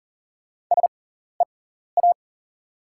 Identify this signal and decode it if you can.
Morse code, heard as a keyed tone, about 40 wpm, spaced as for 14 wpm, 700 Hz. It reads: SEA
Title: 6x Character Spacing - Top 300 Words - 40wpm